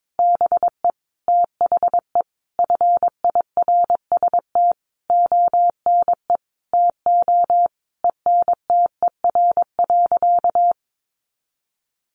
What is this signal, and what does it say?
Morse code, heard as a keyed tone, about 22 wpm, 700 Hz. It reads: BE THE FIRST ONE TO ENTER.
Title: Sentences from Top 1300 Words 22wpm